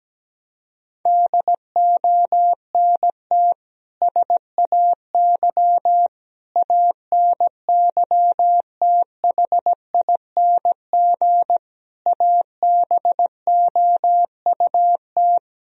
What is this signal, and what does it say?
Morse code, heard as a keyed tone, about 17 wpm, 700 Hz. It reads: DONT SAY ANYTHING ABOUT
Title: Sentences from Top 1000 Words - Review Part 1 - 17wpm